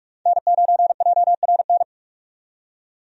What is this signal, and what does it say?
Morse code, heard as a keyed tone, about 47 wpm, 700 Hz. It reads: N9JRN
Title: USA Call Signs - 5 Characters - 45wpm